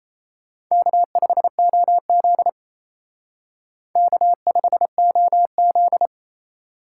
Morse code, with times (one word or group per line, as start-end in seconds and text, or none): 0.7-2.5 K5OZ
4.0-6.1 K5OZ